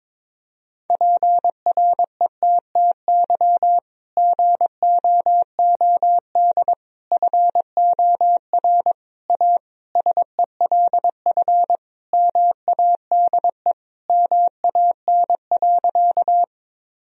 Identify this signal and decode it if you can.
Morse code, heard as a keyed tone, about 22 wpm, 700 Hz. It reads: PRETTY GOOD FOR A SELF MADE MAN.
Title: Sentences from Top 1000 Words - Review Part 1 - 22wpm